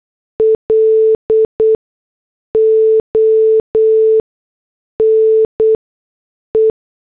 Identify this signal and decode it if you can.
Morse code, heard as a keyed tone, about 8 wpm, spaced as for 6 wpm, 440 Hz. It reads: LONE
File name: morse / headlines